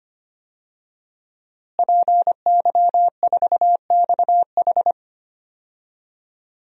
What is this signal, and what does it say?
Morse code, heard as a keyed tone, about 25 wpm, 700 Hz. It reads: PY4XH
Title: DXCC-100 Call Signs - 5 Characters - Repeated 3x - 25wpm